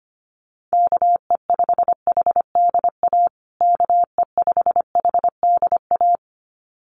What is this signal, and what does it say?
Morse code, heard as a keyed tone, about 25 wpm, 700 Hz. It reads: KE5HDA KE5HDA
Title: USA Call Signs Repeated 2x 25wpm